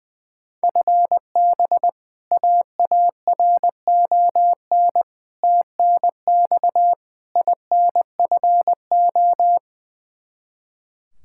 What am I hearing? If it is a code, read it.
Morse code, heard as a keyed tone, about 20 wpm, 700 Hz. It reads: FB AARON TNX INFO
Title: K1USN SST QSO Practice 20wpm